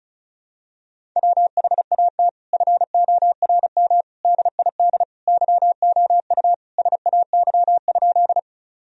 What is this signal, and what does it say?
Morse code, heard as a keyed tone, about 34 wpm, 700 Hz. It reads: WHAT FORM DID YOU SAY?
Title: Sentences from Top 1000 Words - Review Part 1 - 35wpm